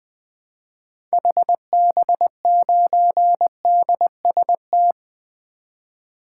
Morse code, heard as a keyed tone, about 20 wpm, 700 Hz. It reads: HB9DST
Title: 2023 SOTA Activator Call Signs - 20wpm